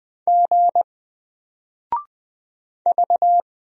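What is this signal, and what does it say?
Morse code, heard as a keyed tone, about 20 wpm, 700 Hz. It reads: G V
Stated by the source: Single letter and number Morse drill at 20 wpm, from N4LCD.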